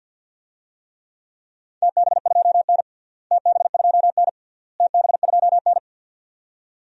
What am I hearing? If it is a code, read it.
Morse code, heard as a keyed tone, about 46 wpm, 700 Hz. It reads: TB2N TB2N TB2N
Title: DXCC-100 Call Signs - 4 Characters - Repeated 3x - 50wpm